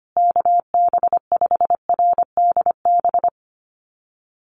Morse code, heard as a keyed tone, about 25 wpm, 700 Hz. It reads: KB5RDB
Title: USA Call Signs Repeated 3x 25wpm